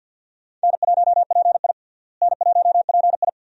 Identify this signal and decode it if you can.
Morse code, heard as a keyed tone, about 51 wpm, 700 Hz. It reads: N1PI N1PI
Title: USA Call Signs - 4 Characters - Repeated 2x - 50wpm